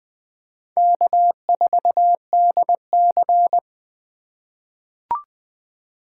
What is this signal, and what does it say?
Morse code, heard as a keyed tone, about 21 wpm, 700 Hz. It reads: K4DC
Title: USA Call Signs - 4 Characters - 6x Character Spacing - 20wpm